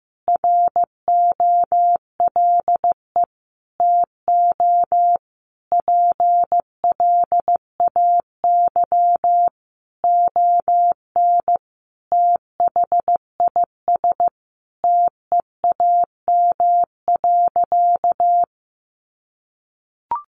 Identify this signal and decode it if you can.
Morse code, heard as a keyed tone, about 15 wpm, 700 Hz. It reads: ROLE TO PLAY ON THIS TEAM.
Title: Sentences from Top 1500 Words 15wpm